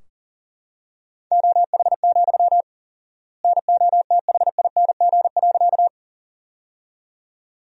Morse code, heard as a keyed tone, about 40 wpm, 700 Hz. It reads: OH, NOTHING.
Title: 4x Word-Spacing - Sentences from Top 500 Words - 40wpm